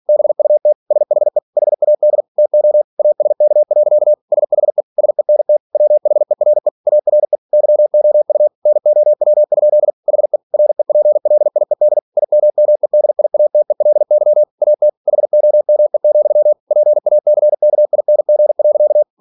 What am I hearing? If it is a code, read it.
Morse code, heard as a keyed tone, about 46 wpm, 600 Hz. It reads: BUT SHE HAD TO ASK. SHE SENT WHERE ARE YOU NOW? HE REPLIED IMMEDIATELY AT HOME, PACKING.
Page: Never 47wpm